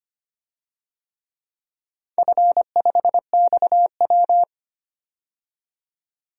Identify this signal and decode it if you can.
Morse code, heard as a keyed tone, about 25 wpm, 700 Hz. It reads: F5XW
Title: DXCC-100 Call Signs - 4 Characters - 25wpm